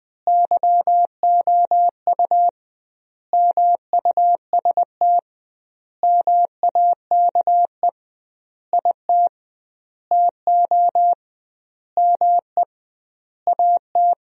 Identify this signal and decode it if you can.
Morse code, heard as a keyed tone, about 20 wpm, 700 Hz. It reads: YOU MUST MAKE IT TO ME AT
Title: 2x Word-Spacing - Sentences from Top 500 Words - 20wpm